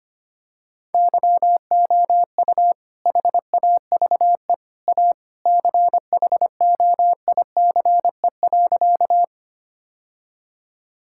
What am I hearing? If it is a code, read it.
Morse code, heard as a keyed tone, about 25 wpm, 700 Hz. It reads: YOU HAVE A CHOICE.
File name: practice / 25wpm / Sentences with 3-Word Phrases - 25wpm